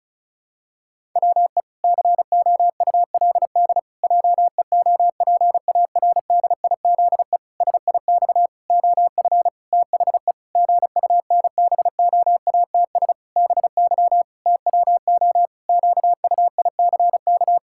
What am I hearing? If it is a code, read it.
Morse code, heard as a keyed tone, about 36 wpm, 700 Hz. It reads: WE COULD JEOPARDIZE SIX OF THE GUNBOATS BY TWO QUICK
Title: Pangram Challenge v2 - 35wpm